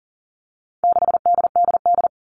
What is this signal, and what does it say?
Morse code, heard as a keyed tone, about 40 wpm, 700 Hz. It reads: 6DDD
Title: All Cut Numbers - 4-Digit 40wpm